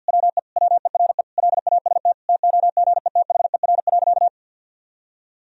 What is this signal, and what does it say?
Morse code, heard as a keyed tone, about 46 wpm, 700 Hz. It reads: WE WERE LAST TOGETHER.
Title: Sentences from Top 1000 Words - Review Part 1 - 50wpm